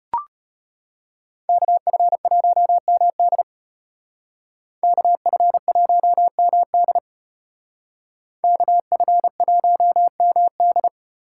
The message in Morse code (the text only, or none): KF1MD KF1MD KF1MD